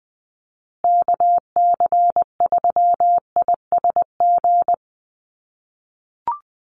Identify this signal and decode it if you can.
Morse code, heard as a keyed tone, about 20 wpm, 700 Hz. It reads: KC3ISG
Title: USA Call Signs Repeated 3x 20wpm